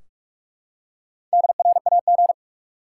Morse code, heard as a keyed tone, about 46 wpm, 700 Hz. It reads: DRAG
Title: Four Letter Words - 45wpm